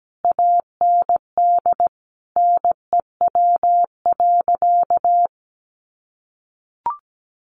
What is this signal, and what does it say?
Morse code, heard as a keyed tone, about 17 wpm, 700 Hz. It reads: AND NEW.
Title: Sentences from Top 1500 Words 17wpm